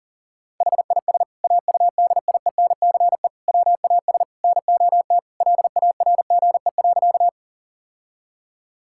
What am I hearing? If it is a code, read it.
Morse code, heard as a keyed tone, about 40 wpm, 700 Hz. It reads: HIS AUDIENCE WAS NOT LARGE.
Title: Sentences from Top 1000 Words - Review Part 5 - 40wpm